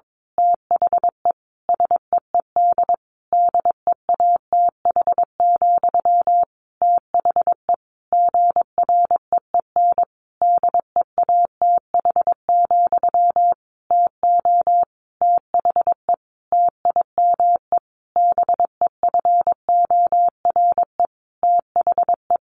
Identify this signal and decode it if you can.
Morse code, heard as a keyed tone, about 22 wpm, 700 Hz. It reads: THE SEED DEATH, THE GREEN DEATH, TO THE TIME BEFORE THE
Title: A Sound Of Thunder 22wpm